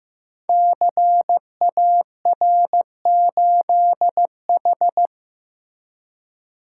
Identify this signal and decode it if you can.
Morse code, heard as a keyed tone, about 15 wpm, 700 Hz. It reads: CAR8H
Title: Code Groups - 5 Letter:Numbers - 15wpm